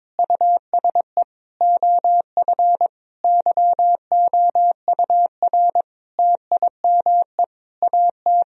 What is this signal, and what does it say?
Morse code, heard as a keyed tone, about 22 wpm, 700 Hz. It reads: USE OF YOUR TIME AT
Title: Sentences from Top 1000 Words - Review Part 1 - 22wpm